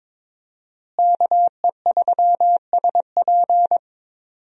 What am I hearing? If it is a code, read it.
Morse code, heard as a keyed tone, about 21 wpm, 700 Hz. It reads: KE3SP
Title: USA Call Signs - 5 Characters - Repeated 3x - 22wpm